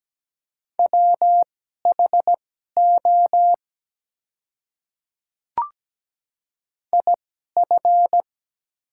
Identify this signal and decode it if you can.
Morse code, heard as a keyed tone, about 17 wpm, spaced as for 12 wpm, 700 Hz. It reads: WHO IF
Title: 2x Character Spacing - Top 100 Words - 17wpm